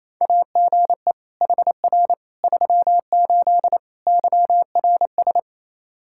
Morse code, heard as a keyed tone, about 28 wpm, 700 Hz. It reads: AGE HR 38 YRS
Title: Practice QSO - Exchange 3 - More Info - 28wpm